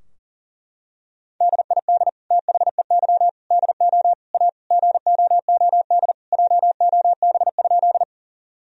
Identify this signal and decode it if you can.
Morse code, heard as a keyed tone, about 40 wpm, 700 Hz. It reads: DID THEY DO A GOOD JOB?
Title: Sentences from Top 1000 Words - Review Part 3 - 40wpm